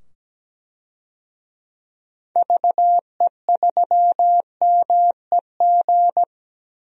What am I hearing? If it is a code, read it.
Morse code, heard as a keyed tone, about 17 wpm, 700 Hz. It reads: VE3MEG
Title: DXCC-100 Call Signs - 6 Characters - 17wpm